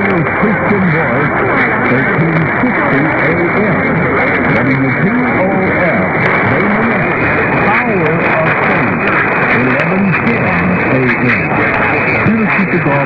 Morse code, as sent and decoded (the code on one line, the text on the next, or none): none